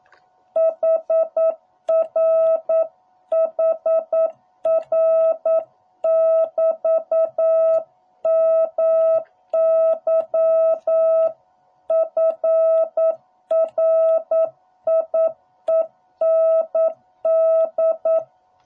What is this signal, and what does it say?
Morse code, heard as a keyed tone, about 9 wpm, 660 Hz. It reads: HR HR <BT> MY FRIEND